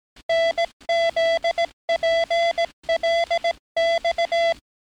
Morse code, heard as a keyed tone, about 18 wpm, 670 Hz. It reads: NZPLX
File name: training / NAVPERS 11049to11060 / NAVPERS 11058 RA RB